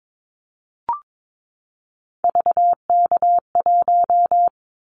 Morse code, heard as a keyed tone, about 22 wpm, 700 Hz. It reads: VK1
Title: Callsign Prefixes for Top 100 DXCC - 22wpm